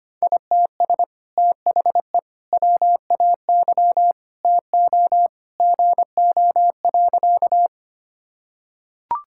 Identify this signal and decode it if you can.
Morse code, heard as a keyed tone, about 25 wpm, 700 Hz. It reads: ITS THE WAY TO GO.
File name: practice / 25wpm / Sentences with 2-Word Phrases - 25wpm